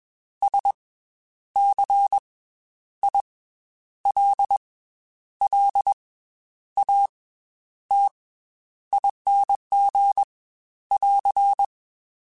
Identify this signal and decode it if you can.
Morse code, heard as a keyed tone, about 21 wpm, 800 Hz. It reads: S C I L L A T ING <AR>